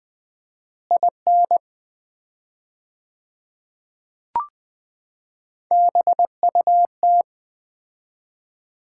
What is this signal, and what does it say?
Morse code, heard as a keyed tone, about 20 wpm, 700 Hz. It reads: IN BUT
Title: 4x Word-Spacing - Sets of 4 Words - 20wpm